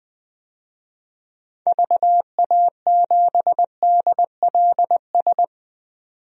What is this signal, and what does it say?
Morse code, heard as a keyed tone, about 20 wpm, 700 Hz. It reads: VA7DLS
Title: DXCC-100 Call Signs - 6 Characters - 20wpm